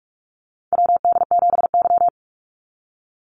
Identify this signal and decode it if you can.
Morse code, heard as a keyed tone, about 44 wpm, 700 Hz. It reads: WD7Y
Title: NA SOTA Chaser Callsigns 45wpm